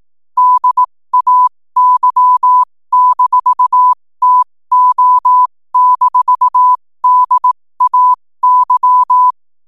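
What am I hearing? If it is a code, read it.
Morse code, heard as a keyed tone, about 18 wpm, spaced as for 15 wpm, 1000 Hz. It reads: DAY-TO-DAY
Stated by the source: 15 wpm Morse code from RNZN Communicators Association.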